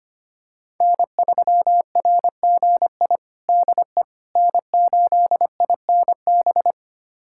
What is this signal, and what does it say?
Morse code, heard as a keyed tone, about 25 wpm, 700 Hz. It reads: N3RGI DE N8INB
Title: USA Callsign DE Callsign - 5 Characters - 25wpm